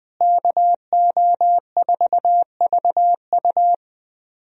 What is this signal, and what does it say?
Morse code, heard as a keyed tone, about 20 wpm, 700 Hz. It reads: KO4VU
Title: USA Call Signs - 5 Characters - Repeated 2x - 20wpm